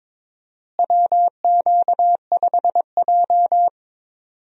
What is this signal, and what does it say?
Morse code, heard as a keyed tone, about 21 wpm, 700 Hz. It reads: WQ5J
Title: USA Call Signs - 4 Characters - Repeated 2x - 22wpm